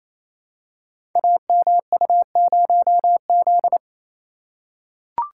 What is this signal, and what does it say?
Morse code, heard as a keyed tone, about 28 wpm, 700 Hz. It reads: AMU0Z
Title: Code Groups - 5 Letter:Numbers - 28wpm